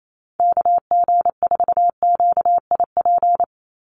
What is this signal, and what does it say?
Morse code, heard as a keyed tone, about 28 wpm, 700 Hz. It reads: KG4QIP
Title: USA Call Signs 28wpm